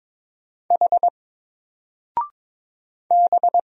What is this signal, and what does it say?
Morse code, heard as a keyed tone, about 22 wpm, 700 Hz. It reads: H B
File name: Single-Letter-Number-22wpm